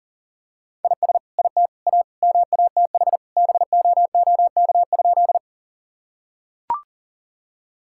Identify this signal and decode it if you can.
Morse code, heard as a keyed tone, about 40 wpm, 700 Hz. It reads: IS IT A MATH BOOK?
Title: Sentences from Top 1000 Words - Review Part 6 - 40wpm